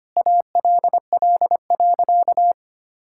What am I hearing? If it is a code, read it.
Morse code, heard as a keyed tone, about 25 wpm, 700 Hz. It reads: ALL.
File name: practice / 25wpm / Sentences with 3-Word Phrases - 25wpm